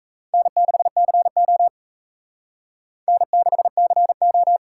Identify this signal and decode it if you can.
Morse code, heard as a keyed tone, about 36 wpm, 700 Hz. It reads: N6CO N6CO